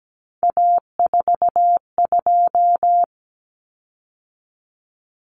Morse code, heard as a keyed tone, about 17 wpm, 690 Hz. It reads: A42
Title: Common Cut Numbers - 3-Digit 17wpm